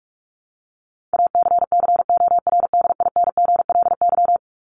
Morse code, heard as a keyed tone, about 44 wpm, 700 Hz. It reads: ACCORDINGLY
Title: LY Suffix - Top 100 Words with Base Words 45wpm